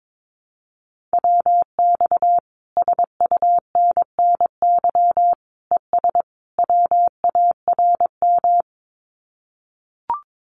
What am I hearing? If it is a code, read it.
Morse code, heard as a keyed tone, about 22 wpm, 700 Hz. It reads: WX SUNNY ES WARM
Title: Common QSO Phrases Encore 22wpm